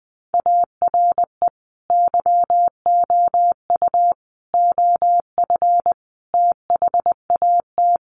Morse code, heard as a keyed tone, about 20 wpm, 700 Hz. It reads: ARE YOU OF THAT
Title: Sentences from Top 1200 Words 20wpm